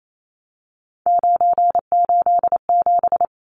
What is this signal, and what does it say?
Morse code, heard as a keyed tone, about 28 wpm, 700 Hz. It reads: 987
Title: Numbers - 3-Digit 28wpm